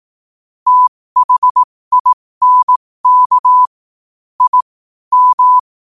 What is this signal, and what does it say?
Morse code, heard as a keyed tone, about 18 wpm, spaced as for 15 wpm, 1000 Hz. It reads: THINK I M
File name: qotdfiles / 15wpm